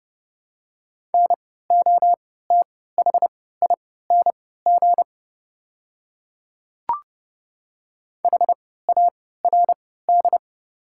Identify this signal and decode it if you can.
Morse code, heard as a keyed tone, about 31 wpm, spaced as for 17 wpm, 690 Hz. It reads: NOTHING HARD
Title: 3x Character Spacing - Top 500 Words - 30wpm